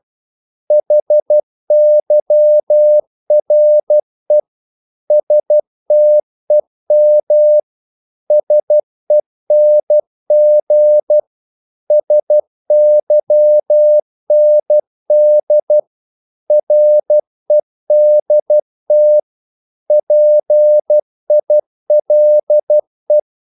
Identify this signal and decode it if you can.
Morse code, heard as a keyed tone, about 12 wpm, 600 Hz. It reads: HYRE STEM SENG SYND REDT PILE
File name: Korte ord DK 12wpm